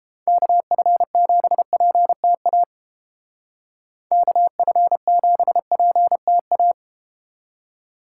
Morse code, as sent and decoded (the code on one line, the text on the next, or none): -.- ..-. --... .--. - .- / -.- ..-. --... .--. - .-
KF7PTA KF7PTA